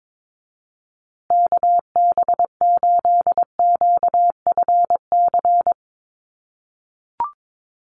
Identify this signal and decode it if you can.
Morse code, heard as a keyed tone, about 22 wpm, 700 Hz. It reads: KB8QFC
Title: USA Call Signs Repeated 2x 22wpm